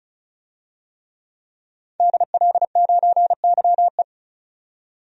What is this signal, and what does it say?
Morse code, heard as a keyed tone, about 36 wpm, 700 Hz. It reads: DL9YE
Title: DXCC-100 Call Signs - 5 Characters - 35wpm